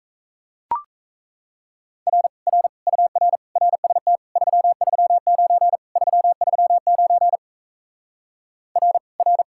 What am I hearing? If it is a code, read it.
Morse code, heard as a keyed tone, about 37 wpm, 700 Hz. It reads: R R UR RST 339 339 R R